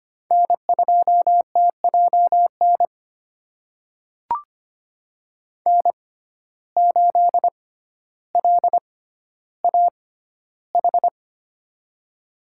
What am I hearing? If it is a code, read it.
Morse code, heard as a keyed tone, about 25 wpm, spaced as for 13 wpm, 700 Hz. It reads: N2TJN N8LAH
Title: USA Call Signs - 6x Character Spacing - 25wpm